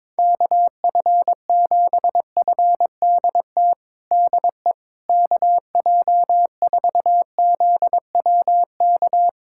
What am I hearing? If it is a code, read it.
Morse code, heard as a keyed tone, about 22 wpm, 700 Hz. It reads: KF7FDT DE KJ4ZWK
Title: USA Callsign DE Callsign - 6 Characters - 22wpm